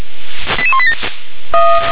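Five-tone selective call (ZVEI), heard as quick runs of five tones, tones of about 70 ms. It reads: 88187
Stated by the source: ZVEI five-tone selective call received on SDR.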